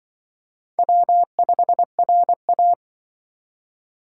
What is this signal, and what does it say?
Morse code, heard as a keyed tone, about 25 wpm, 700 Hz. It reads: W5RA
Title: Speed Racing - USA 1x2 Call Signs 20wpm